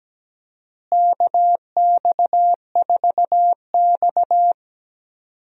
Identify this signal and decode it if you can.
Morse code, heard as a keyed tone, about 17 wpm, 700 Hz. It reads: KX4X
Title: USA Call Signs - 4 Characters - Repeated 3x - 17wpm